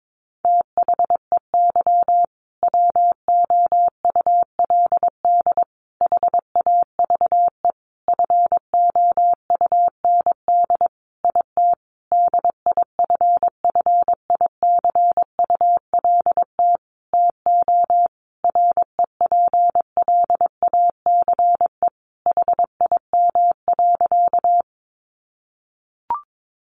Morse code, heard as a keyed tone, about 22 wpm, 700 Hz. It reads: THEY WOULD HAVE FOUND IT DIFFICULT TO REPLACE HIM.
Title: Sentences from Top 1400 Words 22wpm